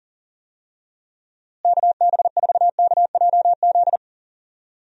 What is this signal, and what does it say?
Morse code, heard as a keyed tone, about 39 wpm, 700 Hz. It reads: KB4KJZ
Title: USA Call Signs - 6 Characters - Repeated 3x - 40wpm